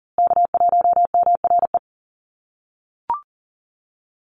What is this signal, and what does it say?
Morse code, heard as a keyed tone, about 40 wpm, 700 Hz. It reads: K1MRE
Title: USA Call Signs Repeated 2x 40wpm